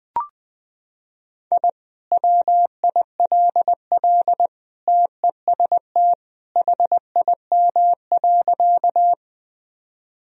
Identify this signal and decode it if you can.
Morse code, heard as a keyed tone, about 20 wpm, 700 Hz. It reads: I WILL TEST HIM.
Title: Sentences from Top 1000 Words - Review Part 3 - 20wpm